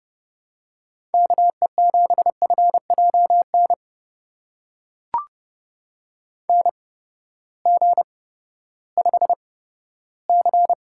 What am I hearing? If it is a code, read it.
Morse code, heard as a keyed tone, about 32 wpm, spaced as for 15 wpm, 700 Hz. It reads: KE7FJN NG5C
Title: USA Call Signs - 8x Character Spacing - 30wpm